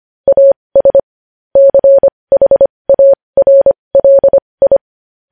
Morse code, heard as a keyed tone, about 25 wpm, spaced as for 20 wpm, 550 Hz. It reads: AS CHARLI
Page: news-headlines-in-morse-code-at-20-wpm